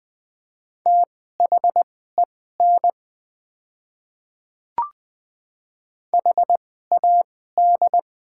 Morse code, heard as a keyed tone, about 20 wpm, spaced as for 14 wpm, 700 Hz. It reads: THEN HAD
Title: 2x Character Spacing - Top 100 Words - 20wpm